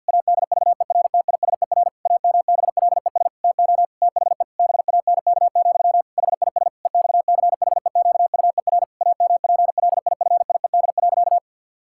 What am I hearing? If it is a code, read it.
Morse code, heard as a keyed tone, about 50 wpm, 700 Hz. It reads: ADVERTISER AMBLES TO THE BANK, HIS EXCHEQUER AMPLIFIED.
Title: Pangram Challenge v2 - 50wpm